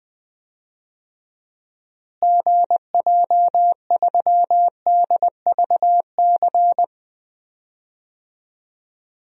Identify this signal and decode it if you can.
Morse code, heard as a keyed tone, about 20 wpm, 700 Hz. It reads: GJ3DVC
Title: DXCC-100 Call Signs - 6 Characters - Repeated 2x - 20wpm